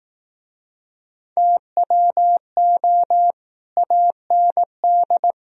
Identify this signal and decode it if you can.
Morse code, heard as a keyed tone, about 20 wpm, 700 Hz. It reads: TWO AND